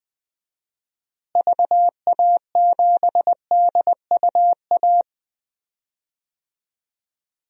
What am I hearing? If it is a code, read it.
Morse code, heard as a keyed tone, about 20 wpm, 700 Hz. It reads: VA7DUA
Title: DXCC-100 Call Signs - 6 Characters - Repeated 2x - 20wpm